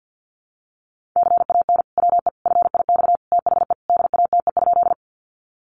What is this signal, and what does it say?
Morse code, heard as a keyed tone, about 50 wpm, 700 Hz. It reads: CAN WE FIX THE DATE?
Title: Sentences from Top 1400 Words 50wpm